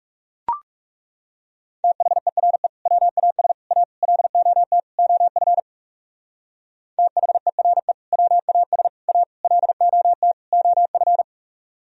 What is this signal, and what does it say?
Morse code, heard as a keyed tone, about 36 wpm, 700 Hz. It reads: THERE WAS A LOT OF THERE WAS A LOT OF
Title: Speed-Racing - Sets of 5 Words 30wpm